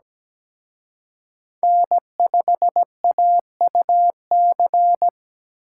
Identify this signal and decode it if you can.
Morse code, heard as a keyed tone, about 20 wpm, 700 Hz. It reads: N5AUC
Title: Speed Racing - USA Call Signs 17wpm